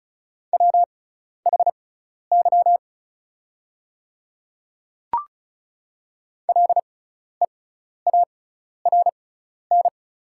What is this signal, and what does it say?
Morse code, heard as a keyed tone, about 36 wpm, spaced as for 12 wpm, 690 Hz. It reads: WHY LEARN
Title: 6x Character Spacing - Top 500 Words - 35wpm